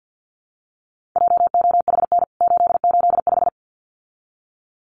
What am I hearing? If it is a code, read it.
Morse code, heard as a keyed tone, about 49 wpm, 700 Hz. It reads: JOHN 885
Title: CWT Exchange 50wpm